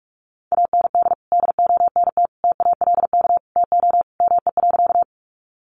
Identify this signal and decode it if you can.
Morse code, heard as a keyed tone, about 44 wpm, 700 Hz. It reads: AND DONT TALK TO ME.
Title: Sentences from Top 400 Words 45wpm